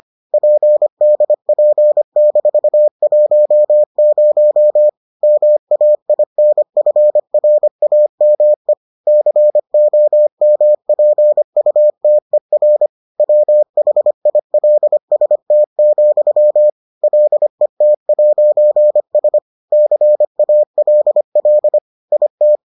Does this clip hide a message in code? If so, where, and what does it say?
Yes: Morse code from 0.3 to 22.6 s, heard as a keyed tone, about 25 wpm, 600 Hz. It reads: PDP-10 MAINFRAME COMPUTER WHILST, LET'S CALL IT